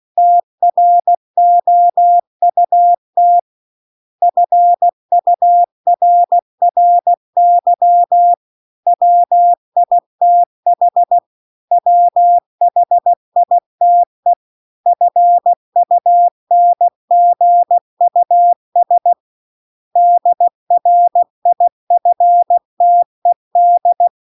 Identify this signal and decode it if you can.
Morse code, heard as a keyed tone, about 16 wpm, 700 Hz. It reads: TROUT FURRY WITH WHITE FUNGUS DRIFTED